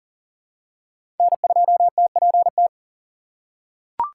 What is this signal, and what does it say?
Morse code, heard as a keyed tone, about 40 wpm, 700 Hz. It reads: N2TPT
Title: USA Call Signs - 6x Character Spacing - 40wpm